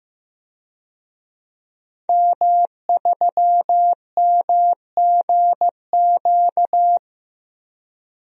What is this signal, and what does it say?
Morse code, heard as a keyed tone, about 15 wpm, 700 Hz. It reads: M3MGQ
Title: DXCC-100 Call Signs - 5 Characters - Repeated 3x - 15wpm